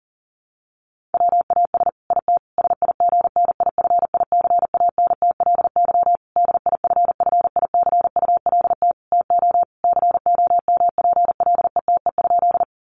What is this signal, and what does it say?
Morse code, heard as a keyed tone, about 40 wpm, 700 Hz. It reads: WAS IT SIGNIFICANTLY DIFFICULT TO COMPLETE?
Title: Sentences from Top 1500 Words 40wpm